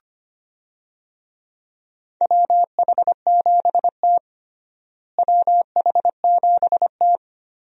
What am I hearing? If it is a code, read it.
Morse code, heard as a keyed tone, about 25 wpm, 700 Hz. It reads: WH7T WH7T
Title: USA Call Signs - 4 Characters - Repeated 2x - 25wpm